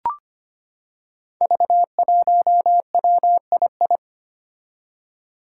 Morse code, heard as a keyed tone, about 25 wpm, 700 Hz. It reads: V1WII
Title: Five Challenging Letter-Number Combinations 25wpm